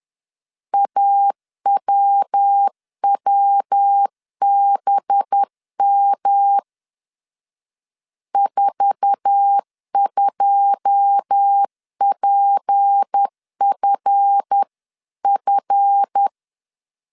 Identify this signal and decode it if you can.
Morse code, heard as a keyed tone, about 10 wpm, 800 Hz. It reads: AWWBM 42PFF
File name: CW / DET / 1 9 1500C S A 10wpm 4511